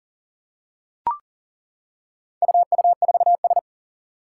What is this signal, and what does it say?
Morse code, heard as a keyed tone, about 40 wpm, 700 Hz. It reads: UU4S
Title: Four Challenging Letter-Number Combinations 40wpm